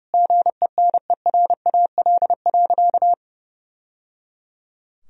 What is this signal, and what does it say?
Morse code, heard as a keyed tone, about 30 wpm, 700 Hz. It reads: GENERAL.
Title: Sentences from Top 1000 Words - Review Part 3 - 30wpm